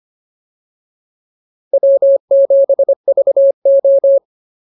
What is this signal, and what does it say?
Morse code, heard as a keyed tone, about 25 wpm, 550 Hz. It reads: W7VO